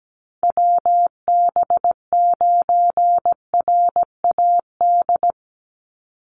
Morse code, heard as a keyed tone, about 17 wpm, 700 Hz. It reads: WB9RAD
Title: USA Call Signs 17wpm